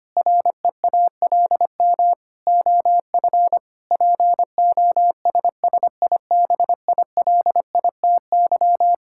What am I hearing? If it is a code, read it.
Morse code, heard as a keyed tone, about 25 wpm, 700 Hz. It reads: REALM OF POSSIBILITY
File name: Sets of 9 Words - 25wpm